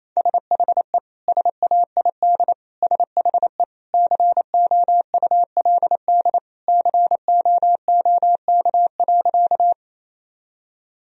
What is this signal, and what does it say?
Morse code, heard as a keyed tone, about 28 wpm, 700 Hz. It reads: SHE SAID SHE COULD COOK.
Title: Sentences from Top 1000 Words - Review Part 4 - 28wpm